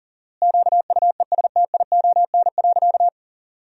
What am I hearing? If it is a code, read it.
Morse code, heard as a keyed tone, about 37 wpm, 700 Hz. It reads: QUESTION.
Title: 4x Word-Spacing - Sentences from Top 400 Words - 40wpm